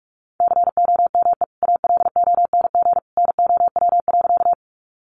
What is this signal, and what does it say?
Morse code, heard as a keyed tone, about 44 wpm, 700 Hz. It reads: COME ALONG NOW.
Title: Sentences from Top 700 Words 45wpm